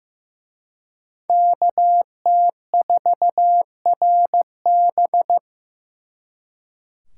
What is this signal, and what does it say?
Morse code, heard as a keyed tone, about 18 wpm, 700 Hz. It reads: KT4RB
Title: Speed Racing - USA Call Signs 15wpm